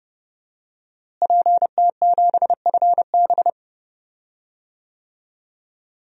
Morse code, heard as a keyed tone, about 30 wpm, 700 Hz. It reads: PT7FB
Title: DXCC-100 Call Signs - 5 Characters - Repeated 2x - 30wpm